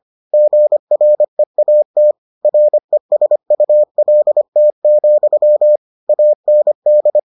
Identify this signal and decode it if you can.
Morse code, heard as a keyed tone, about 25 wpm, 600 Hz. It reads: GREAT RESULT, AND